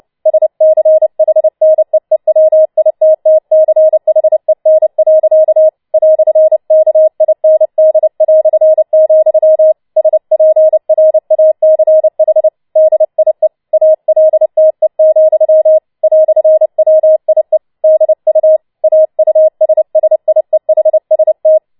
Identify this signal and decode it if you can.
Morse code, heard as a keyed tone, about 29 wpm, 630 Hz. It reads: SCHNEEWITTCHEN. "KIND", SPRACH DIE ALTE, "WIE DU AUSSIEHST
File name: Schneewittchen Teil3 30wpm